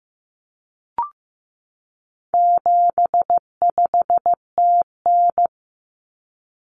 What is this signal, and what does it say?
Morse code, heard as a keyed tone, about 15 wpm, 700 Hz. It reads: 75TN
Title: Common Cut Numbers - 4-Digit 15wpm